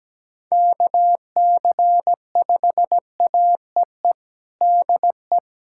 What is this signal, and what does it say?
Morse code, heard as a keyed tone, about 17 wpm, 700 Hz. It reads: KC5AEE DE
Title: USA Callsign DE Callsign - 6 Characters - 17wpm